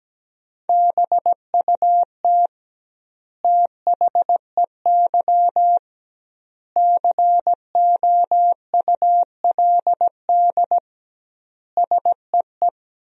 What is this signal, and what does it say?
Morse code, heard as a keyed tone, about 17 wpm, 700 Hz. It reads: BUT THEY COULD SEE
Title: 2x Word-Spacing - Sentences from Top 500 Words - 17wpm